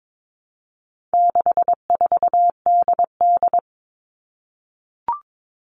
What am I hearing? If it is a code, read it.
Morse code, heard as a keyed tone, about 22 wpm, 700 Hz. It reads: 64DD
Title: All Cut Numbers - 4-Digit 22wpm